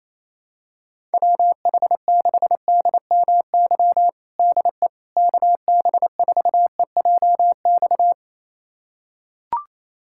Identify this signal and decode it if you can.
Morse code, heard as a keyed tone, about 28 wpm, 700 Hz. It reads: WH6DMY DE KB4EJX
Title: USA Callsign DE Callsign - 6 Characters - 28wpm